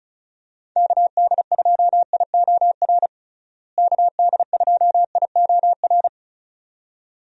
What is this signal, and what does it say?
Morse code, heard as a keyed tone, about 36 wpm, 700 Hz. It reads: KD2IOR KD2IOR
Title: USA Call Signs - 6 Characters - Repeated 2x - 35wpm